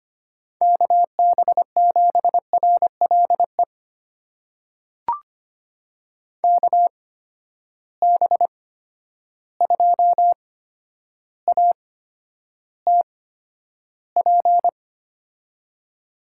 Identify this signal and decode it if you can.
Morse code, heard as a keyed tone, about 25 wpm, spaced as for 11 wpm, 700 Hz. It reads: KB7RLE KB2ATP
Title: USA Call Signs - 6 Characters - 8x Character Spacing - 25wpm